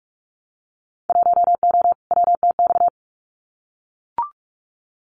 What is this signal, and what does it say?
Morse code, heard as a keyed tone, about 44 wpm, 700 Hz. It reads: 1O WTX
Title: Winter Field Day Exchanges Encore 45wpm